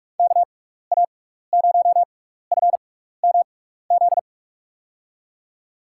Morse code, heard as a keyed tone, about 44 wpm, spaced as for 15 wpm, 700 Hz. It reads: KA0FMZ
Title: USA Call Signs - 6 Characters - 6x Character Spacing - 45wpm